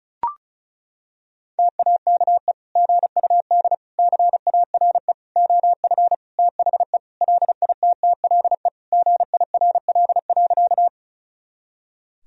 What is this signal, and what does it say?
Morse code, heard as a keyed tone, about 36 wpm, 700 Hz. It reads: TAKE GUD CARE OF THE LITTLE GIRL.
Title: Sentences from Top 400 Words and Abbreviations 35wpm